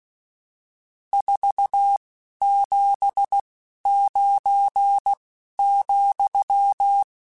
Morse code, heard as a keyed tone, about 16 wpm, 800 Hz. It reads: 4 7 9 ,
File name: RSTs / RST-mp3s / Various RST @10